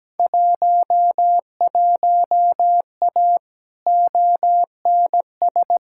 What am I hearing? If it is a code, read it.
Morse code, heard as a keyed tone, about 17 wpm, 700 Hz. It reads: 11A ONS
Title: Field Day Exchanges Encore - 17wpm